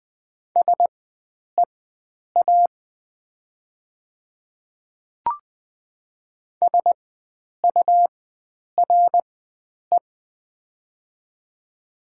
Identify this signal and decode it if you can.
Morse code, heard as a keyed tone, about 20 wpm, spaced as for 9 wpm, 700 Hz. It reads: SEA SURE
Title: 4x Character Spacing - Top 300 Words - 20wpm